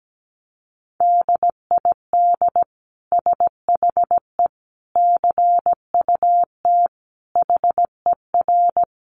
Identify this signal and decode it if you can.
Morse code, heard as a keyed tone, about 17 wpm, 700 Hz. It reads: DID SHE CUT HER
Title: Sentences from Top 1300 Words 17wpm